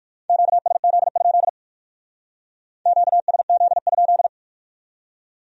Morse code, heard as a keyed tone, about 54 wpm, 700 Hz. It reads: QSZ? QSZ?
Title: Speed-Racing Q-Codes - 45wpm